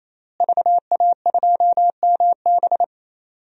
Morse code, heard as a keyed tone, about 28 wpm, 700 Hz. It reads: VA2MB
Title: DXCC-100 Call Signs - 5 Characters - Repeated 3x - 28wpm